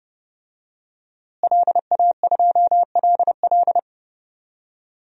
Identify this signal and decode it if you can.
Morse code, heard as a keyed tone, about 30 wpm, 700 Hz. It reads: LA2LL
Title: DXCC-100 Call Signs - 5 Characters - Repeated 2x - 30wpm